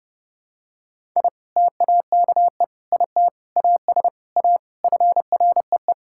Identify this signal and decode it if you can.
Morse code, heard as a keyed tone, about 30 wpm, 700 Hz. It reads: I TAKE IT AS A FREE
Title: Sentences from Top 1000 Words - Review Part 6 - 30wpm